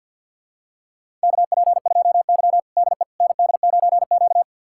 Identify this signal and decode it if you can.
Morse code, heard as a keyed tone, about 52 wpm, 700 Hz. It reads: KP2Y DE ND9Q
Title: USA Callsign DE Callsign - 4 Characters - 50wpm